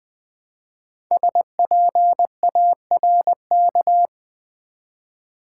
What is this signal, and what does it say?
Morse code, heard as a keyed tone, about 20 wpm, 700 Hz. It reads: SPARK
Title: Build A Word - 1 Letter At A Time - 5-Letter Words - 20wpm